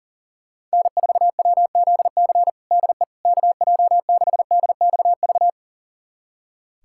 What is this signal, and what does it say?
Morse code, heard as a keyed tone, about 37 wpm, 700 Hz. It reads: N4WZC DE KJ6DXV
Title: USA Callsign DE Callsign 40wpm